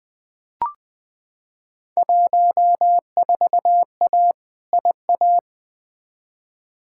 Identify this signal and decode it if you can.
Morse code, heard as a keyed tone, about 20 wpm, 700 Hz. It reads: 14A IA
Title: Field Day Exchanges Encore - 20wpm